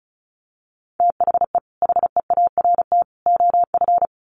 Morse code, heard as a keyed tone, about 34 wpm, 700 Hz. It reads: THE HEART OF
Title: Sets of 3 Words 35wpm